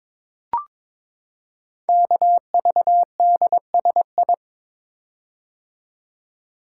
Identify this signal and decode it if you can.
Morse code, heard as a keyed tone, about 22 wpm, 700 Hz. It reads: KVDSI
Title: Code Groups - 5 Letters - 22wpm